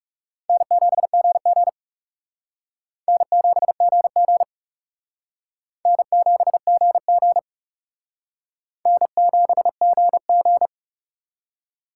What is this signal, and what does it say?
Morse code, heard as a keyed tone, about 37 wpm, 700 Hz. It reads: N7GG N7GG N7GG N7GG
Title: Speed Racing - USA 1x2 Call Signs 30wpm